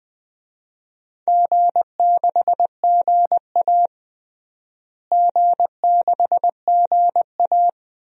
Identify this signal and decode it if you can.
Morse code, heard as a keyed tone, about 20 wpm, 700 Hz. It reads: G6GA G6GA
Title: DXCC-100 Call Signs - 4 Characters - Repeated 2x - 20wpm